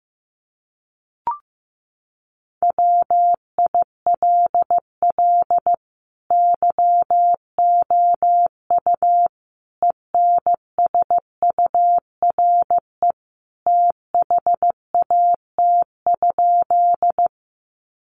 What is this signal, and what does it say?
Morse code, heard as a keyed tone, about 15 wpm, 700 Hz. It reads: WILL YOU ENSURE THAT?
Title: Sentences from Top 1100 Words 15wpm